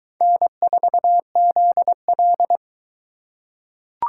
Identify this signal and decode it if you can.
Morse code, heard as a keyed tone, about 22 wpm, 700 Hz. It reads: N4ZL